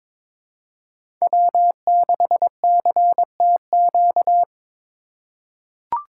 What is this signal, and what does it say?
Morse code, heard as a keyed tone, about 22 wpm, 700 Hz. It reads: W6CTQ
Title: USA Call Signs - 5 Characters - 4x Character Spacing - 22wpm